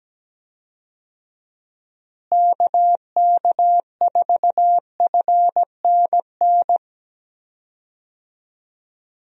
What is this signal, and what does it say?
Morse code, heard as a keyed tone, about 17 wpm, 700 Hz. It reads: KK4FNN
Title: USA Call Signs - 6 Characters - Repeated 2x - 17wpm